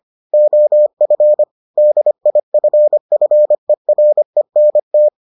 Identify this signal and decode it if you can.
Morse code, heard as a keyed tone, about 25 wpm, 600 Hz. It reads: OF DIFFERENT